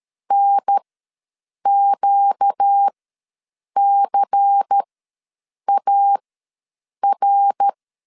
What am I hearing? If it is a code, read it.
Morse code, heard as a keyed tone, about 13 wpm, spaced as for 7 wpm, 800 Hz. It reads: NQCAR
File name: LEZIONI / 5 LBVC / 5 LBVC ws 12wpm 3159